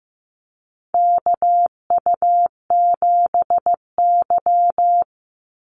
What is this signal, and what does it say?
Morse code, heard as a keyed tone, about 15 wpm, 700 Hz. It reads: KU7Y
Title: Call Signs Repeated 2x - 15wpm